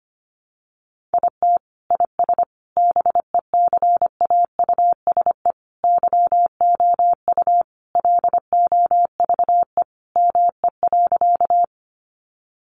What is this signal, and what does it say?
Morse code, heard as a keyed tone, about 25 wpm, 700 Hz. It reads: IT IS BECAUSE YOU LOVE ME.
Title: Sentences from Top 600 Words 25wpm